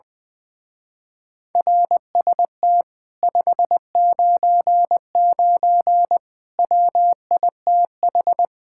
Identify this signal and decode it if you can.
Morse code, heard as a keyed tone, about 20 wpm, 700 Hz. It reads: RST 599 WITH
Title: Speed-Racing Common QSO Phrases - 20wpm